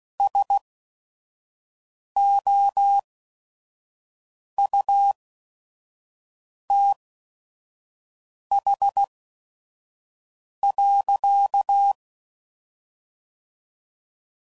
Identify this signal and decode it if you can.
Morse code, heard as a keyed tone, about 16 wpm, spaced as for 5 wpm, 800 Hz. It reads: SOUTH.
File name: CW Code Groups @5WPM / Weather Sets / Weather 4 10